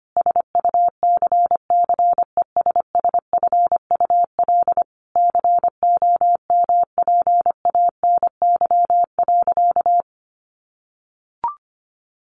Morse code, heard as a keyed tone, about 25 wpm, 690 Hz. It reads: SUCCESSFUL COMPANY.
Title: Sentences from Top 1100 Words 25wpm